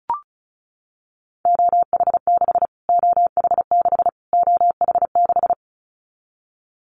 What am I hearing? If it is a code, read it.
Morse code, heard as a keyed tone, about 34 wpm, 700 Hz. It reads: OH6 OH6 OH6
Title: Callsign Prefixes for Top 25 DXCC - Repeated 3x - 35wpm